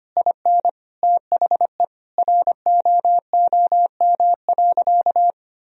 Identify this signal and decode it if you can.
Morse code, heard as a keyed tone, about 25 wpm, 700 Hz. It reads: IN THE ROOM.
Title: Sentences from Top 1000 Words - Review Part 3 - 25wpm